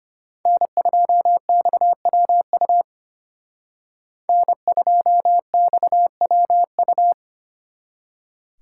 Morse code, heard as a keyed tone, about 32 wpm, 700 Hz. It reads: N2XWU N2XWU